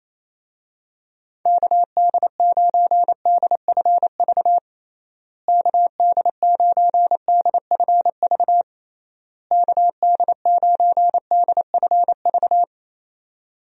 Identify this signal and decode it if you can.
Morse code, heard as a keyed tone, about 28 wpm, 700 Hz. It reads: KD9DFV KD9DFV KD9DFV
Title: USA Call Signs - 6 Characters - Repeated 3x - 28wpm